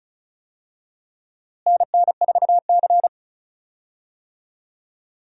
Morse code, heard as a keyed tone, about 36 wpm, 700 Hz. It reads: NN4C
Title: USA Call Signs - 4 Characters - Repeated 3x - 35wpm